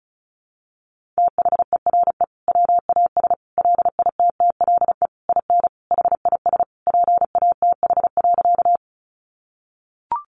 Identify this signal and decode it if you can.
Morse code, heard as a keyed tone, about 34 wpm, 700 Hz. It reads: THERE WAS LITTLE IN HIS PATH.
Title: Sentences from Top 1400 Words 35wpm